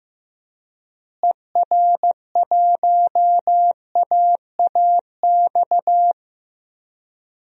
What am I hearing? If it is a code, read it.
Morse code, heard as a keyed tone, about 15 wpm, 700 Hz. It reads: ER1AAX
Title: DXCC-100 Call Signs - 6 Characters - Repeated 2x - 15wpm